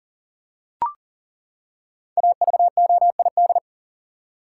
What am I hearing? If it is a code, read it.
Morse code, heard as a keyed tone, about 40 wpm, 700 Hz. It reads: AVOID
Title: Five Letter Words - 40wpm